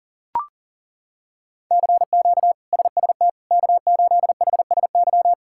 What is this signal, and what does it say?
Morse code, heard as a keyed tone, about 40 wpm, 700 Hz. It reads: CQ SST K8HSY
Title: K1USN SST QSO Practice - 40wpm